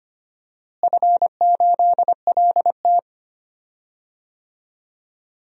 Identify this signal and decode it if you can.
Morse code, heard as a keyed tone, about 25 wpm, 700 Hz. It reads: F8LT
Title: DXCC-100 Call Signs - 4 Characters - Repeated 2x - 25wpm